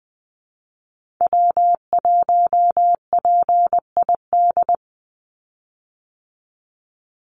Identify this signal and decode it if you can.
Morse code, heard as a keyed tone, about 20 wpm, 700 Hz. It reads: W1PID
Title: Call Signs Repeated 3x - 20wpm